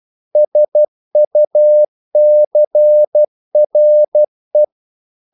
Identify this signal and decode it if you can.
Morse code, heard as a keyed tone, about 12 wpm, 600 Hz. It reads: SUCRE
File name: CW / Practice / NationalCapitals12